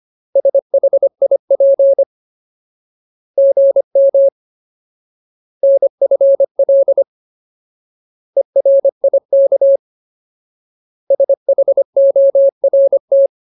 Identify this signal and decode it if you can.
Morse code, heard as a keyed tone, about 25 wpm, 550 Hz. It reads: SHIP GM NFL ERIK SHORT
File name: Ultimate Word List / 25WPM Ultimate